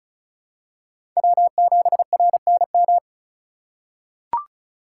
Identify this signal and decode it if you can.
Morse code, heard as a keyed tone, about 36 wpm, 700 Hz. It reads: W7RNM
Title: USA Call Signs - 5 Characters - 2x Character Spacing - 35wpm